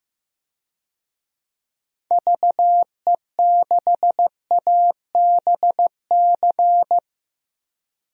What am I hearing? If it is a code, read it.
Morse code, heard as a keyed tone, about 16 wpm, 700 Hz. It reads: VE6ABC
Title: DXCC-100 Call Signs - 6 Characters - 15wpm